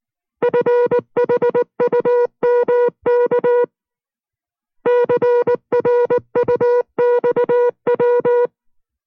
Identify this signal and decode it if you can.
Morse code, heard as a keyed tone, about 19 wpm, 480 Hz. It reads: FHUMK CRUXW